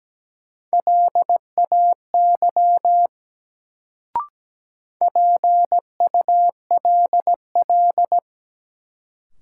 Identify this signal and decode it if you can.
Morse code, heard as a keyed tone, about 17 wpm, 700 Hz. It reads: LAY PULL
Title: Sentences from Top 400 Words - New Word Drill - 17wpm